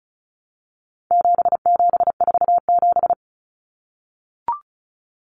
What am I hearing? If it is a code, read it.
Morse code, heard as a keyed tone, about 34 wpm, 700 Hz. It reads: 7747
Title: Common Cut Numbers - 4-Digit 35wpm